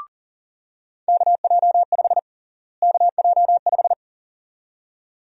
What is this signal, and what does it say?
Morse code, heard as a keyed tone, about 40 wpm, 700 Hz. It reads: KJ5 KJ5
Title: USA Call Sign Prefixes - Repeated 2x - 40wpm